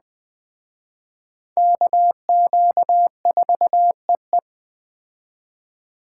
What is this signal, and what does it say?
Morse code, heard as a keyed tone, about 21 wpm, 700 Hz. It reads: KQ4EE
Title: Speed Racing - USA 2x2 Call Signs 17wpm